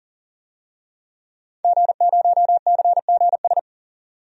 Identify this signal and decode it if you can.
Morse code, heard as a keyed tone, about 40 wpm, 700 Hz. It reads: G0CGS
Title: DXCC-100 Call Signs - 5 Characters - 40wpm